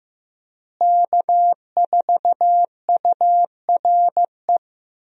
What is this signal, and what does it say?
Morse code, heard as a keyed tone, about 16 wpm, 700 Hz. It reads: K4URE
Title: USA Call Signs - 5 Characters - Repeated 3x - 15wpm